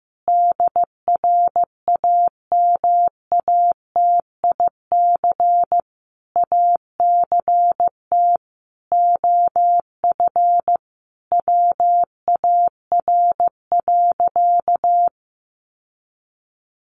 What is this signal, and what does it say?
Morse code, heard as a keyed tone, about 15 wpm, 700 Hz. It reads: DRAMATIC ACT OF WAR.
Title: Sentences from Top 1300 Words 15wpm